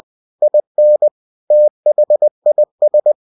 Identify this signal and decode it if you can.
Morse code, heard as a keyed tone, about 20 wpm, 600 Hz. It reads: IN THIS
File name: cw-books / winnie-the-pooh / winnie-20wpm-page-0119